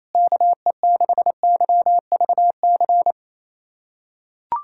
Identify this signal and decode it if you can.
Morse code, heard as a keyed tone, about 28 wpm, 700 Hz. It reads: KE6YVC
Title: USA Callsign DE Callsign - 6 Characters - 28wpm